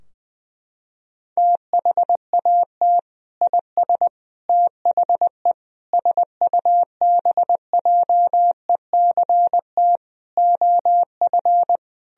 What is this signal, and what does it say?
Morse code, heard as a keyed tone, about 20 wpm, 700 Hz. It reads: THAT IS THE SUBJECT OF
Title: Sentences from Top 1000 Words - Review Part 4 - 20wpm